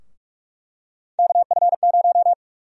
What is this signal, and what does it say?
Morse code, heard as a keyed tone, about 45 wpm, 700 Hz. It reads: KF0
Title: USA Call Sign Prefixes - 45wpm